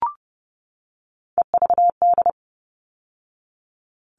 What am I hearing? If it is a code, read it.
Morse code, heard as a keyed tone, about 30 wpm, 700 Hz. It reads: EVD
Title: All Cut Numbers - 3-Digit 30wpm